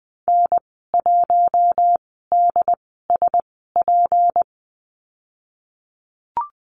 Morse code, heard as a keyed tone, about 20 wpm, spaced as for 15 wpm, 700 Hz. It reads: N1DSP
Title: 154 Lesson 25g - Letter p - Call Signs 20wpm